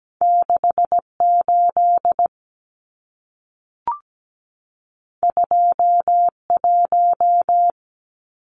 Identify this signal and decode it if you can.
Morse code, heard as a keyed tone, about 17 wpm, 700 Hz. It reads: 68 21
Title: Numbers - 2-Digit 17wpm